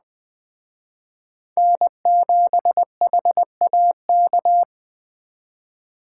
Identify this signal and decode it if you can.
Morse code, heard as a keyed tone, about 21 wpm, 700 Hz. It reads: N7HAK